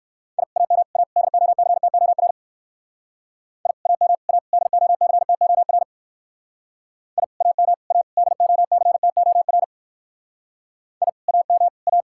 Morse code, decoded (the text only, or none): I AM A DOCTOR I AM A DOCTOR I AM A DOCTOR I AM A